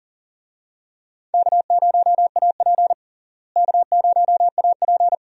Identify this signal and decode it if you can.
Morse code, heard as a keyed tone, about 40 wpm, 700 Hz. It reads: K0AP K0AP
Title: USA Call Signs - 4 Characters - Repeated 2x - 40wpm